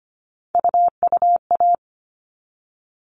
Morse code, heard as a keyed tone, about 25 wpm, 700 Hz. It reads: UUA
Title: All Cut Numbers - 3-Digit 25wpm